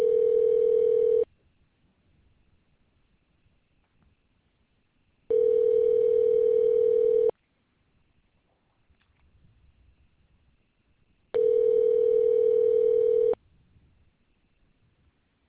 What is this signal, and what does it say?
Ringback tone (North America), 440 + 480 Hz.